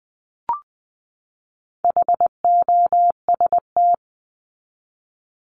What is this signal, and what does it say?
Morse code, heard as a keyed tone, about 20 wpm, 700 Hz. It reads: HOST
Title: Top 1501-1750 Words 20wpm